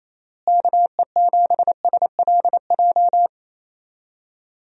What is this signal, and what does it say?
Morse code, heard as a keyed tone, about 28 wpm, 700 Hz. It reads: KE7SLJ
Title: USA Call Signs - 6 Characters - Repeated 3x - 28wpm